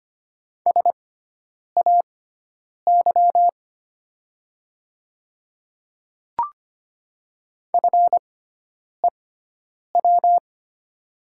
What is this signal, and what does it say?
Morse code, heard as a keyed tone, about 25 wpm, spaced as for 9 wpm, 700 Hz. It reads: SAY FEW
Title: 6x Character Spacing - Top 300 Words - 25wpm